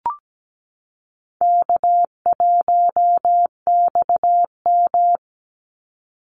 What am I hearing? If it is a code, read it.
Morse code, heard as a keyed tone, about 17 wpm, 700 Hz. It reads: K1XM
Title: Easy Call Signs Repeated 2x - 17wpm